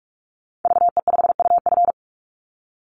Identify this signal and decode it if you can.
Morse code, heard as a keyed tone, about 44 wpm, 700 Hz. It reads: VE5UF
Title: Call Signs Repeated 3x - 45wpm